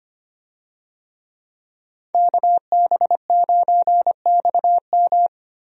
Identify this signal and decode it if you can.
Morse code, heard as a keyed tone, about 25 wpm, 700 Hz. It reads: KB9XM
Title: USA Call Signs - 5 Characters - Repeated 3x - 25wpm